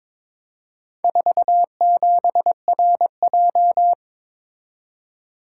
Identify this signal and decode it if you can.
Morse code, heard as a keyed tone, about 22 wpm, 700 Hz. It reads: 47RJ
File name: practice / 22wpm / Four Letter-Number Combinations - 22wpm